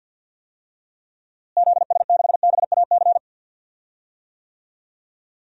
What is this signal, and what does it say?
Morse code, heard as a keyed tone, about 51 wpm, 700 Hz. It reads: ZS6BAC
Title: DXCC-100 Call Signs - 6 Characters - Repeated 3x - 50wpm